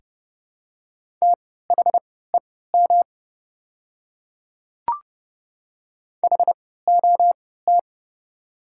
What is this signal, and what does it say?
Morse code, heard as a keyed tone, about 31 wpm, spaced as for 17 wpm, 700 Hz. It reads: THEM HOT
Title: 3x Character Spacing - Top 300 Words - 30wpm